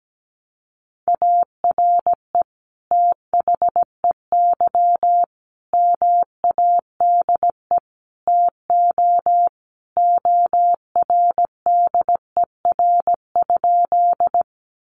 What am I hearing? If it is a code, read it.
Morse code, heard as a keyed tone, about 17 wpm, 700 Hz. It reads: ARE THEY MADE TO ORDER?
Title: Sentences from Top 400 Words 17wpm